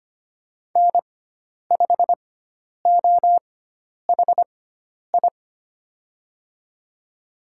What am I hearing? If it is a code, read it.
Morse code, heard as a keyed tone, about 25 wpm, spaced as for 10 wpm, 700 Hz. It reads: N5OHI
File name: Lesson 16g - Letter c - Call Signs 25wpm